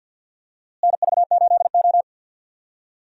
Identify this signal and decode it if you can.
Morse code, heard as a keyed tone, about 51 wpm, 700 Hz. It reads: NV8O
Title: USA Call Signs - 4 Characters - Repeated 3x - 50wpm